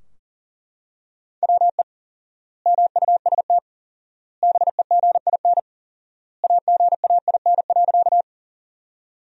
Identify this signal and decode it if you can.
Morse code, heard as a keyed tone, about 40 wpm, 700 Hz. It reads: WE MUST BEGIN AGAIN.
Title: 4x Word-Spacing - Sentences from Top 300 Words - 40wpm